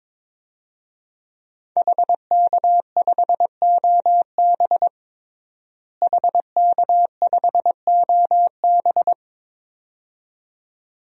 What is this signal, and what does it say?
Morse code, heard as a keyed tone, about 22 wpm, 700 Hz. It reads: HK5OB HK5OB
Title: DXCC-100 Call Signs - 5 Characters - Repeated 2x - 22wpm